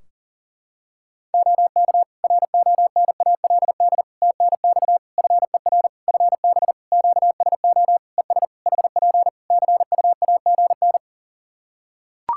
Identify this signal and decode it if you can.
Morse code, heard as a keyed tone, about 40 wpm, 700 Hz. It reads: OK RONALD TNX FER FB QSO ES HP CUAGN
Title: Practice QSO - Exchange 4 - Ending - 40wpm